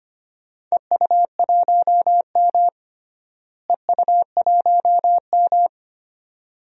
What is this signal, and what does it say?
Morse code, heard as a keyed tone, about 25 wpm, 700 Hz. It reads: EU1M EU1M
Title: DXCC-100 Call Signs - 4 Characters - Repeated 2x - 25wpm